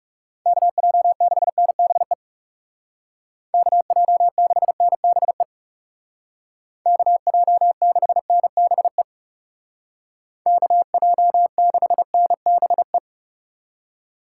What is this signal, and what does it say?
Morse code, heard as a keyed tone, about 36 wpm, 700 Hz. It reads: KJ6NBE KJ6NBE KJ6NBE KJ6NBE
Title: Speed Racing - USA 2x3 Call Signs 30wpm